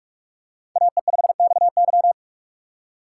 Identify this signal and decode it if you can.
Morse code, heard as a keyed tone, about 48 wpm, 700 Hz. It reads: AE5XY
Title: USA Call Signs - 5 Characters - 2x Character Spacing - 45wpm